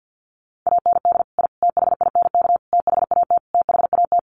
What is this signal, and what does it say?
Morse code, heard as a keyed tone, about 50 wpm, 700 Hz. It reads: AND I THINK THAT THAT
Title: Sets of 5 Words Encore 50wpm